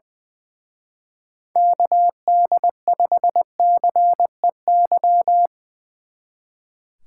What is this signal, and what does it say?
Morse code, heard as a keyed tone, about 23 wpm, 700 Hz. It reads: KD5CEY
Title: Speed Racing - USA 2x3 Call Signs 20wpm